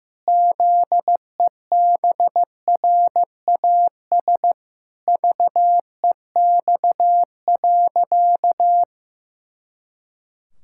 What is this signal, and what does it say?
Morse code, heard as a keyed tone, about 15 wpm, 700 Hz. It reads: ZEBRAS VEX.
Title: Pangram Challenge v2 - 15wpm